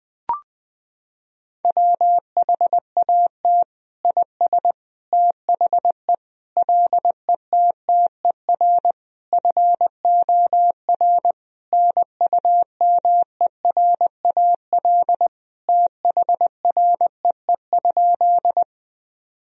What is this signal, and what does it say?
Morse code, heard as a keyed tone, about 20 wpm, 700 Hz. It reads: WHAT IS THE LETTER FOR NUMERAL THREE?
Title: Sentences from Top 1000 Words - Review Part 2 - 20wpm